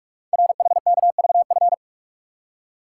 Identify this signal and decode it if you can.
Morse code, heard as a keyed tone, about 44 wpm, 690 Hz. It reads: RHKVF
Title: Code Groups - 5 Letters - 45wpm